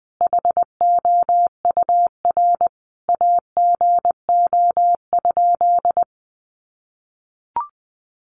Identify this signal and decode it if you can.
Morse code, heard as a keyed tone, about 20 wpm, 700 Hz. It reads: HOUR AGO?
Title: Sentences from Top 1300 Words 20wpm